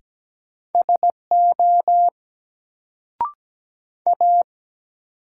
Morse code, heard as a keyed tone, about 17 wpm, 700 Hz. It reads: SO A
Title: Sentences from Top 100 Words - New Word Drill - 17wpm